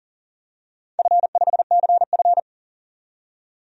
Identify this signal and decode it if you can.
Morse code, heard as a keyed tone, about 37 wpm, 700 Hz. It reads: F5CF
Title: DXCC-100 Call Signs - 4 Characters - 40wpm